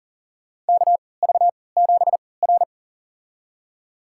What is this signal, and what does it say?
Morse code, heard as a keyed tone, about 42 wpm, spaced as for 23 wpm, 700 Hz. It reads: KV7R
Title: USA Call Signs - 4 Characters - 3x Character Spacing - 40wpm